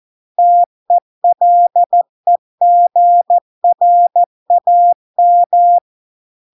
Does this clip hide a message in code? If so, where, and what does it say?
Yes: Morse code from 0.4 to 5.8 s, heard as a keyed tone, about 14 wpm, 700 Hz. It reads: TELEGRAM